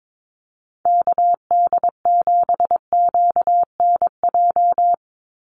Morse code, heard as a keyed tone, about 22 wpm, 700 Hz. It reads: KD7QNJ
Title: USA Call Signs 22wpm